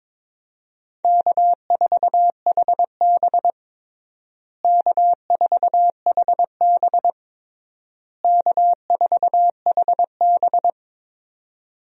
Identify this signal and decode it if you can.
Morse code, heard as a keyed tone, about 22 wpm, 700 Hz. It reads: K4HB K4HB K4HB
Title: USA Call Signs - 4 Characters - Repeated 3x - 22wpm